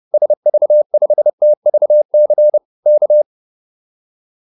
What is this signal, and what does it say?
Morse code, heard as a keyed tone, about 30 wpm, 600 Hz. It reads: SV5TVC K